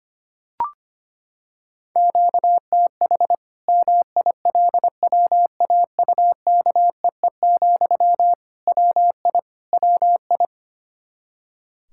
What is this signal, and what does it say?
Morse code, heard as a keyed tone, about 25 wpm, 700 Hz. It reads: QTH MILWAUKEE, WI WI
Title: K1USN SST QSO Practice 25wpm